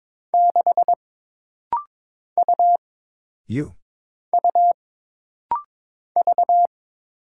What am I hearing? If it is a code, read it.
Morse code, heard as a keyed tone, about 22 wpm, 700 Hz. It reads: 6 U U V
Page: This rapid-fire timing is geared to push your ICR (Instant Character Recognition) skills.